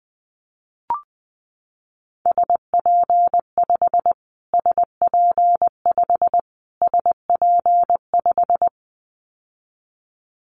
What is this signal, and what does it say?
Morse code, heard as a keyed tone, about 20 wpm, 700 Hz. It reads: SP5 SP5 SP5
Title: Callsign Prefixes for Top 25 DXCC - Repeated 3x - 20wpm